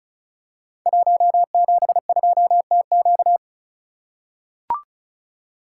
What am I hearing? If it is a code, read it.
Morse code, heard as a keyed tone, about 36 wpm, 690 Hz. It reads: 172TQ
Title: Code Groups - 5 Letter:Numbers - 35wpm